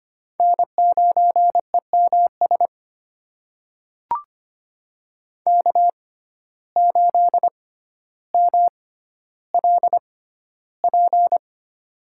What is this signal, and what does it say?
Morse code, heard as a keyed tone, about 25 wpm, spaced as for 13 wpm, 700 Hz. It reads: N9EMS K8MLP
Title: USA Call Signs - 5 Characters - 6x Character Spacing - 25wpm